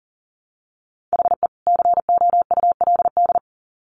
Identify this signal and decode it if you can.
Morse code, heard as a keyed tone, about 40 wpm, 700 Hz. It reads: HE COULD
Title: Sets of 2 Words 40wpm